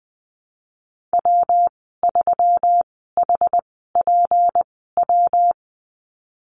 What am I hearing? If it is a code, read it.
Morse code, heard as a keyed tone, about 20 wpm, spaced as for 15 wpm, 700 Hz. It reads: W3HPW
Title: 179 Lesson 29g - Slash - Call Signs 20wpm